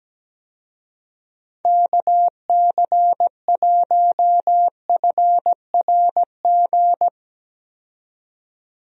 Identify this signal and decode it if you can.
Morse code, heard as a keyed tone, about 17 wpm, 700 Hz. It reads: KC1FRG
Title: USA Call Signs - 6 Characters - 17wpm